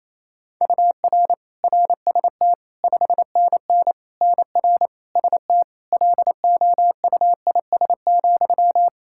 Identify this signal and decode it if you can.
Morse code, heard as a keyed tone, about 28 wpm, 700 Hz. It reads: UR RST 5NN NR ST LOUIS,
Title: Practice QSO - Exchange 1 - Beginning - 28wpm